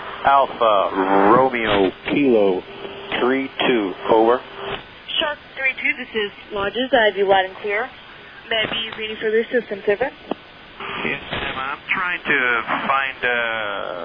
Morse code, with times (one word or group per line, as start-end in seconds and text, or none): none